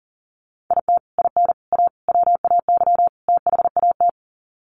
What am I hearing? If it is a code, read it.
Morse code, heard as a keyed tone, about 40 wpm, 700 Hz. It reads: IT IN A WAY THAT
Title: Sets of 5 Words Encore2 40wpm